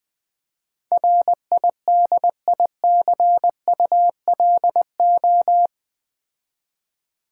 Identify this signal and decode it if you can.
Morse code, heard as a keyed tone, about 20 wpm, 700 Hz. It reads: RIDICULO
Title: Build A Word - 4 Letters At A Time - 10-Letter Words - 20wpm